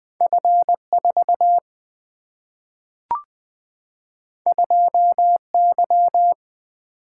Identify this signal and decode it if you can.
Morse code, heard as a keyed tone, about 20 wpm, 700 Hz. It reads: F 4 2 Y
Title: Single-Letter Single-Number 20wpm